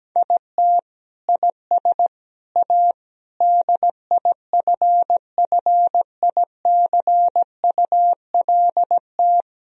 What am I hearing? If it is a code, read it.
Morse code, heard as a keyed tone, about 17 wpm, 700 Hz. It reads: IT IS A DIFFICULT
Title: Sentences from Top 1000 Words - Review Part 3 - 17wpm